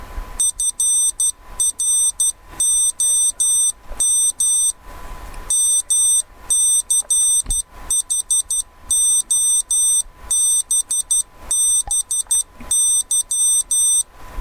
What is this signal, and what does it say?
Morse code, heard as a keyed tone, about 12 wpm, 3600 Hz. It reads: FROM MCHOBBY
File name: arduino-morse / arduino-morse-hello-from-mch